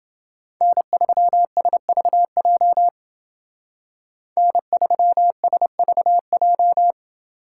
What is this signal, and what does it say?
Morse code, heard as a keyed tone, about 28 wpm, 700 Hz. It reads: N3SVJ N3SVJ